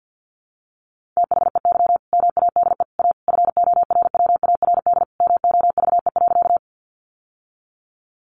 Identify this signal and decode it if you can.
Morse code, heard as a keyed tone, about 50 wpm, 700 Hz. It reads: THEY MADE A FORWARD MOVE.
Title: Sentences from Top 1300 Words 50wpm